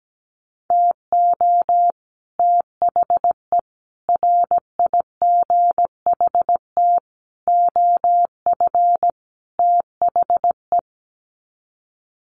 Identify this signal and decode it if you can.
Morse code, heard as a keyed tone, about 17 wpm, 700 Hz. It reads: TO THE RIGHT OF THE
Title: Sets of 5 Words Encore 17wpm